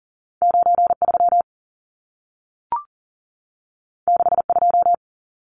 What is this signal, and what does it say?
Morse code, heard as a keyed tone, about 40 wpm, 700 Hz. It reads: 93 62
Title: Numbers - 2-Digit 40wpm